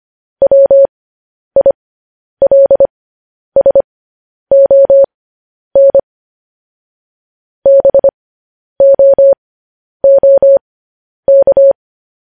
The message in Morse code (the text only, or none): WILSON BOOK